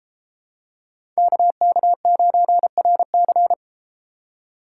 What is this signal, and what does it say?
Morse code, heard as a keyed tone, about 32 wpm, 700 Hz. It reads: KK9RC